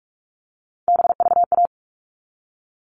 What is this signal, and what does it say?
Morse code, heard as a keyed tone, about 44 wpm, 700 Hz. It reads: BVA
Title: All Cut Numbers - 3-Digit 45wpm